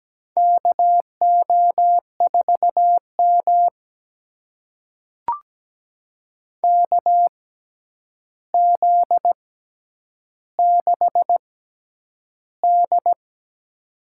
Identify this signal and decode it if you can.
Morse code, heard as a keyed tone, about 17 wpm, spaced as for 9 wpm, 700 Hz. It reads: KO4M KZ6D
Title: USA Call Signs - 4 Characters - 6x Character Spacing - 17wpm